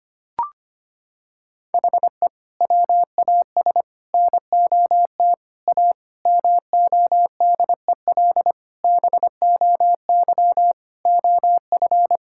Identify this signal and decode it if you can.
Morse code, heard as a keyed tone, about 25 wpm, 700 Hz. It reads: HE WAS NOT A MODEL BOY OF
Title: Sentences from Top 1000 Words - Review Part 5 - 25wpm